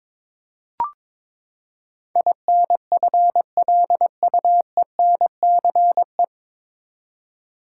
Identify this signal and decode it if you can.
Morse code, heard as a keyed tone, about 22 wpm, 700 Hz. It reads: INFLUENCE
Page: Morse Code Ninja practice 22wpm